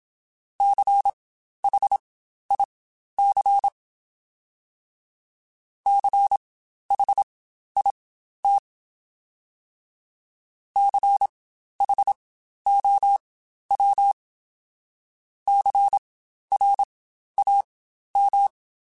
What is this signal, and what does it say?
Morse code, heard as a keyed tone, about 27 wpm, spaced as for 12 wpm, 800 Hz. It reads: CHIC CHIT CHOW CRAM